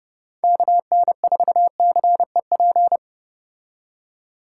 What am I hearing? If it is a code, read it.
Morse code, heard as a keyed tone, about 30 wpm, 700 Hz. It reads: KN4CEP